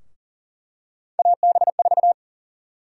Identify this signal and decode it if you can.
Morse code, heard as a keyed tone, about 40 wpm, 700 Hz. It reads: AB4
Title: USA Call Sign Prefixes - Repeated 2x - 40wpm